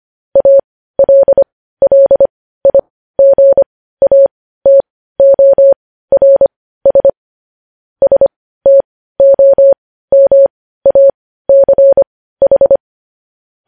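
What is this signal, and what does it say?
Morse code, heard as a keyed tone, about 25 wpm, spaced as for 15 wpm, 550 Hz. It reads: ALLIGATORS STOMACH
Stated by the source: Morse code news headlines sent at 15.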